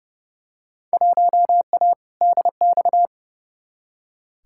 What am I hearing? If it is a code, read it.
Morse code, heard as a keyed tone, about 30 wpm, 700 Hz. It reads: 1A DX
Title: Field Day Exchanges Encore - 30wpm